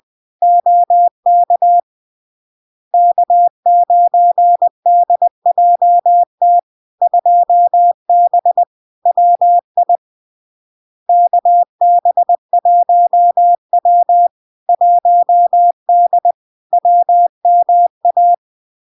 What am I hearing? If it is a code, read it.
Morse code, heard as a keyed tone, about 20 wpm, 700 Hz. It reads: OK K9DJT 2B WI KB1W 1D WMA
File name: LARK-FD-2022-Log-in-CW-20wpm-700-Hz